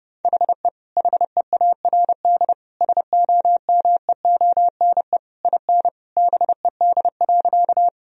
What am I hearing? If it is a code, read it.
Morse code, heard as a keyed tone, about 32 wpm, 700 Hz. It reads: HE HEARD SOMEONE IN BED.
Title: Sentences from Top 1000 Words - Review Part 4 - 30wpm